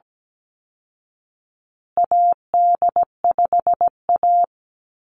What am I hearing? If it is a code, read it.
Morse code, heard as a keyed tone, about 17 wpm, 700 Hz. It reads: AD5A
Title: Call Signs Repeated 2x - 17wpm